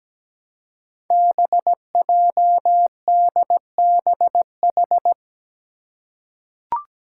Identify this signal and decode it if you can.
Morse code, heard as a keyed tone, about 17 wpm, 700 Hz. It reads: BJDBH
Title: Five Challenging Letter-Number Combinations 17wpm